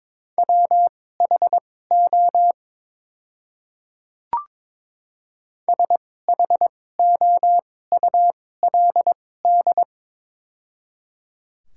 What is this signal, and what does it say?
Morse code, heard as a keyed tone, about 21 wpm, spaced as for 16 wpm, 700 Hz. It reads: WHO SHOULD
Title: 2x Character Spacing - Top 500 Words - 22wpm